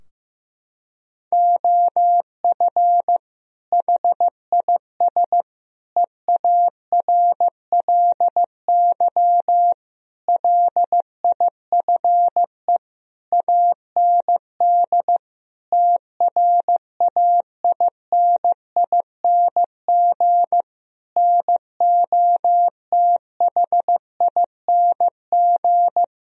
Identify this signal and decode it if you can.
Morse code, heard as a keyed tone, about 15 wpm, 700 Hz. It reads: OF HIS EARLY LIFE AND TRAINING NOTHING
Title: Sentences from Top 1000 Words - Review Part 4 - 15wpm